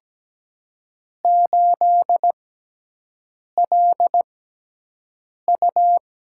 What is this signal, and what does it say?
Morse code, heard as a keyed tone, about 17 wpm, spaced as for 6 wpm, 700 Hz. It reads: 8LU
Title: USA Call Signs - 6x Character Spacing - 17wpm